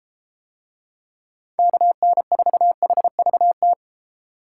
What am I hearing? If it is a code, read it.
Morse code, heard as a keyed tone, about 32 wpm, 700 Hz. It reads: KN4HVT